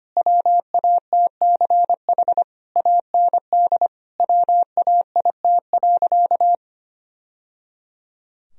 Morse code, heard as a keyed tone, about 25 wpm, 700 Hz. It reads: WATCH AND WAIT.
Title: Sentences from Top 1000 Words - Review Part 4 - 25wpm